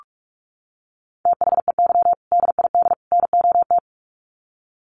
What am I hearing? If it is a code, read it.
Morse code, heard as a keyed tone, about 44 wpm, 700 Hz. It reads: THEY DID NOT
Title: Sets of 3 Words 45wpm